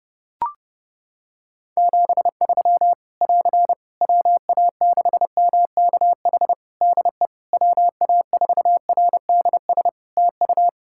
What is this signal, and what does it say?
Morse code, heard as a keyed tone, about 30 wpm, 700 Hz. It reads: 73 <AR> WA6MKH DE WA4RDS TU
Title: Practice QSO - Exchange 4 - Ending - 30wpm